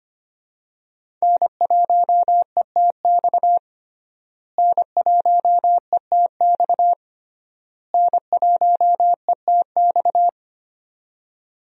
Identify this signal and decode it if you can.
Morse code, heard as a keyed tone, about 25 wpm, 700 Hz. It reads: N1ETX N1ETX N1ETX
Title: USA Call Signs - 5 Characters - Repeated 3x - 25wpm